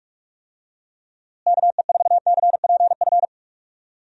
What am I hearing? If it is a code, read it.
Morse code, heard as a keyed tone, about 46 wpm, 700 Hz. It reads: KE4CPF
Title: USA Call Signs - 6 Characters - Repeated 2x - 45wpm